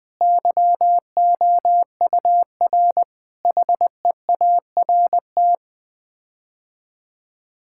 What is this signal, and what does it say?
Morse code, heard as a keyed tone, about 20 wpm, 700 Hz. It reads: YOUR HEART
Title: Sets of 9 Words - 20wpm